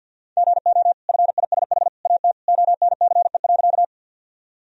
Morse code, heard as a keyed tone, about 48 wpm, 700 Hz. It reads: GO FISH AT ONCE.
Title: Sentences from Top 1000 Words - Review Part 1 - 50wpm